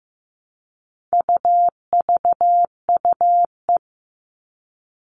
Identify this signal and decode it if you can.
Morse code, heard as a keyed tone, about 15 wpm, 700 Hz. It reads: UVUE
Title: All Cut Numbers - 4-Digit 15wpm